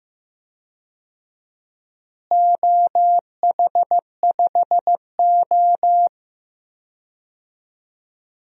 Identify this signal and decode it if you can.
Morse code, heard as a keyed tone, about 15 wpm, 700 Hz. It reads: OH5O
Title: DXCC-100 Call Signs - 4 Characters - Repeated 3x - 15wpm